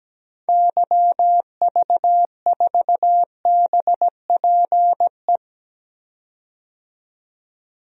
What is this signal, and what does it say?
Morse code, heard as a keyed tone, about 17 wpm, 700 Hz. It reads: YV4BPE
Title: DXCC-100 Call Signs - 6 Characters - 17wpm